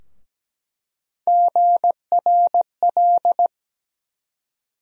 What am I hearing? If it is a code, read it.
Morse code, heard as a keyed tone, about 17 wpm, 700 Hz. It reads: GRL
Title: USA Call Sign Postfixes - 3 Letters 17wpm